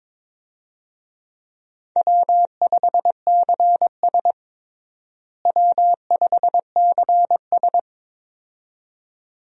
Morse code, heard as a keyed tone, about 22 wpm, 700 Hz. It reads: W5CS W5CS
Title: USA Call Signs - 4 Characters - Repeated 2x - 22wpm